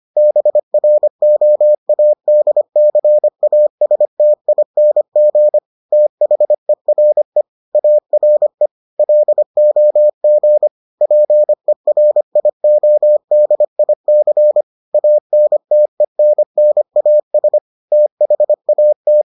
Morse code, heard as a keyed tone, about 25 wpm, 600 Hz. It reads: BROADCASTING THERE ARE LOG PERIODIC ANTENNAS THAT